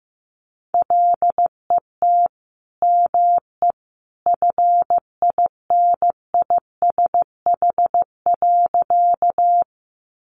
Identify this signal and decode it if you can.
Morse code, heard as a keyed tone, about 15 wpm, 700 Hz. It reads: LET ME FINISH.
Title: Sentences from Top 1200 Words 15wpm